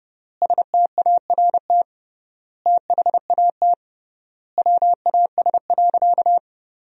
Morse code, heard as a keyed tone, about 30 wpm, 700 Hz. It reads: START THAT WAS.
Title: 3x Word-Spacing - Sentences from Top 300 Words - 30wpm